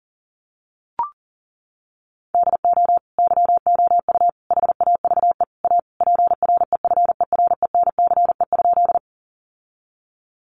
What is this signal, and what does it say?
Morse code, heard as a keyed tone, about 37 wpm, 700 Hz. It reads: DO YOU HAVE A PREFERENCE?
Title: Sentences from Top 1500 Words 40wpm